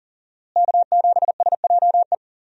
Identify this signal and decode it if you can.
Morse code, heard as a keyed tone, about 40 wpm, 700 Hz. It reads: K7SJE
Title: USA Call Signs - 5 Characters - Repeated 2x - 40wpm